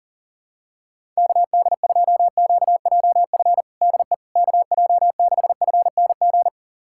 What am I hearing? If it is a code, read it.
Morse code, heard as a keyed tone, about 40 wpm, 700 Hz. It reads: KD2QJF DE KJ6FNG
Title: USA Callsign DE Callsign - 6 Characters - 40wpm